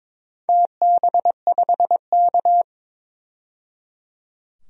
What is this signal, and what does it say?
Morse code, heard as a keyed tone, about 23 wpm, 700 Hz. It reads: TB5K
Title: DXCC-100 Call Signs - 4 Characters - 22wpm